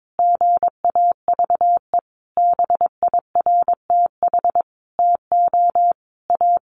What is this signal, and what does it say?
Morse code, heard as a keyed tone, about 22 wpm, 700 Hz. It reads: GAVE BIRTH TO A
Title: Sentences from Top 1100 Words 22wpm